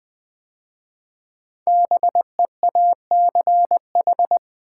Morse code, heard as a keyed tone, about 20 wpm, 700 Hz. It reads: BEACH
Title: 5 Letter Words - First 3 Letters Given - 20wpm